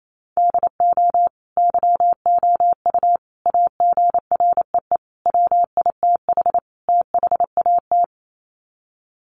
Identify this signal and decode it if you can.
Morse code, heard as a keyed tone, about 28 wpm, 700 Hz. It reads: DO YOU AGREE WITH THAT
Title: Sets of 5 Words Encore 28wpm